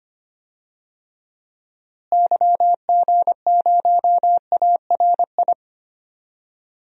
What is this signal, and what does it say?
Morse code, heard as a keyed tone, about 25 wpm, 700 Hz. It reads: YG0ARI
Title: DXCC-100 Call Signs - 6 Characters - Repeated 3x - 25wpm